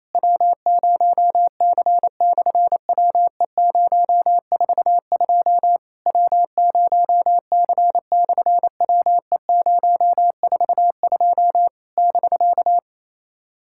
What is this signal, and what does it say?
Morse code, heard as a keyed tone, about 28 wpm, 700 Hz. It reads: W0C/WE042 W0C/WE042 <BK>
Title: SOTA QSO Practice 28wpm